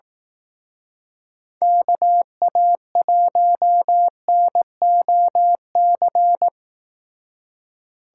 Morse code, heard as a keyed tone, about 18 wpm, 700 Hz. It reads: KA1NOC
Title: Speed Racing - USA 2x3 Call Signs 15wpm